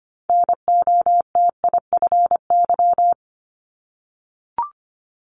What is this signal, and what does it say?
Morse code, heard as a keyed tone, about 25 wpm, 700 Hz. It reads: NOTIFY
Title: Multi-letter Phonemes 25wpm